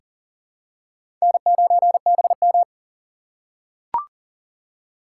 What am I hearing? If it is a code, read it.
Morse code, heard as a keyed tone, about 37 wpm, 700 Hz. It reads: N9BM
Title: USA Call Signs - 4x Character Spacing - 40wpm